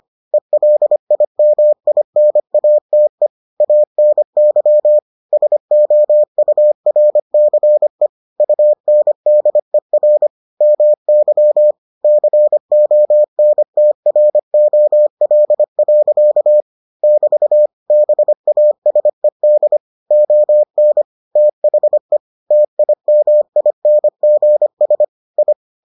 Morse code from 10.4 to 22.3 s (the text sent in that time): MY CONTROL. <BT> BASED ON THE